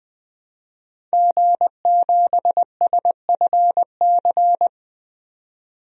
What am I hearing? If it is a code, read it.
Morse code, heard as a keyed tone, about 20 wpm, 700 Hz. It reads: G7SFC
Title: DXCC-100 Call Signs - 5 Characters - 20wpm